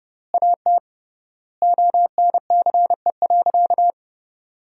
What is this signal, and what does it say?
Morse code, heard as a keyed tone, about 30 wpm, 700 Hz. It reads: AT ONCE.
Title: 3x Word-Spacing - Sentences from Top 500 Words - 30wpm